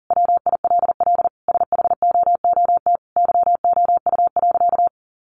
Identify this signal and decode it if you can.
Morse code, heard as a keyed tone, about 40 wpm, 700 Hz. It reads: WILL SHOOT YOU.
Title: Sentences from Top 1200 Words 40wpm